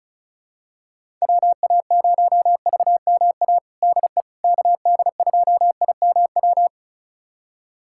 Morse code, heard as a keyed tone, about 34 wpm, 700 Hz. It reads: WA0VMA DE KD2IMW
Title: USA Callsign DE Callsign 35wpm